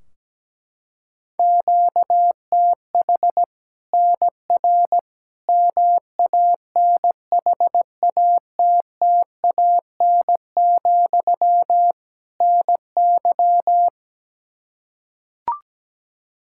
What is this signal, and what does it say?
Morse code, heard as a keyed tone, about 17 wpm, 700 Hz. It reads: QTH NR MANHATTAN, NY
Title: Practice QSO - Exchange 1 - Beginning - 17wpm